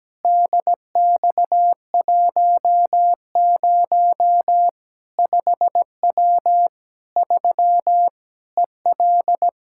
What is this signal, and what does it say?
Morse code, heard as a keyed tone, about 17 wpm, 700 Hz. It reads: DX10 5W 3 EL
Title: Practice QSO - Exchange 2 - Info - 17wpm